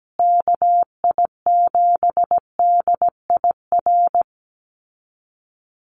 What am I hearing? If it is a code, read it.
Morse code, heard as a keyed tone, about 17 wpm, 700 Hz. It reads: KI7DIR
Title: USA Call Signs Repeated 3x 17wpm